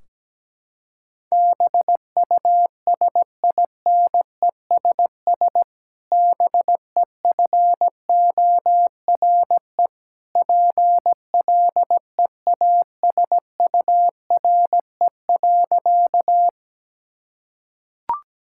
Morse code, heard as a keyed tone, about 17 wpm, 700 Hz. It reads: BUSINESS BEFORE PLEASURE.
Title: Sentences from Top 1600 Words 17wpm